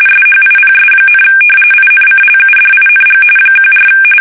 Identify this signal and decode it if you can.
RTTY, about 45.45 baud. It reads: SHIPS, INCLUDING 14
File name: fsk850 / usnavy3